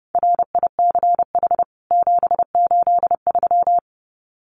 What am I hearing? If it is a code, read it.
Morse code, heard as a keyed tone, about 30 wpm, 700 Hz. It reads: RICH 783
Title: CWT Exchange 30wpm